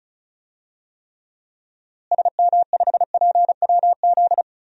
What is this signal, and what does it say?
Morse code, heard as a keyed tone, about 36 wpm, 700 Hz. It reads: SM5PWZ
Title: DXCC-100 Call Signs - 6 Characters - Repeated 3x - 35wpm